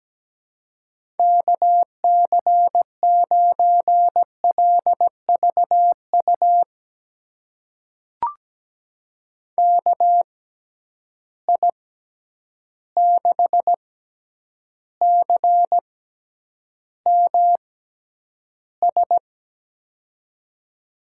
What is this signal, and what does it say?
Morse code, heard as a keyed tone, about 17 wpm, spaced as for 9 wpm, 700 Hz. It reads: KC9LVU KI6CMS
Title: USA Call Signs - 6 Characters - 6x Character Spacing - 17wpm